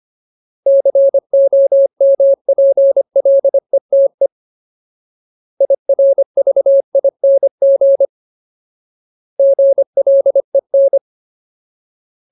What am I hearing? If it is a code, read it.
Morse code, heard as a keyed tone, about 25 wpm, 550 Hz. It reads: COMPLETE IRVING GLEN
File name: Ultimate Word List / 25WPM Ultimate